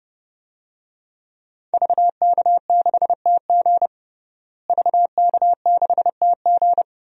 Morse code, heard as a keyed tone, about 32 wpm, 700 Hz. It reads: VK6TG VK6TG
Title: DXCC-100 Call Signs - 5 Characters - Repeated 3x - 30wpm